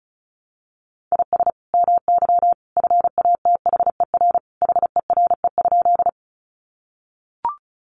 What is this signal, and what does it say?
Morse code, heard as a keyed tone, about 34 wpm, 700 Hz. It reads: IS MY FATHER HERE?
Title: Sentences from Top 300 Words 35wpm